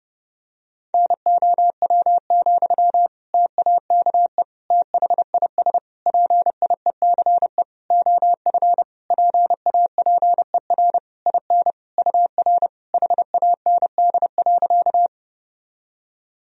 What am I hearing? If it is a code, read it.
Morse code, heard as a keyed tone, about 32 wpm, 700 Hz. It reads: NOW, TAKE THIS PIECE OF PAPER IN UR HAND.
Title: Sentences from Top 400 Words and Abbreviations 30wpm